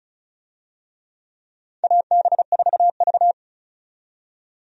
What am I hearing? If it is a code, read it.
Morse code, heard as a keyed tone, about 36 wpm, 700 Hz. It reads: AB4V
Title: USA Call Signs - 4 Characters - 35wpm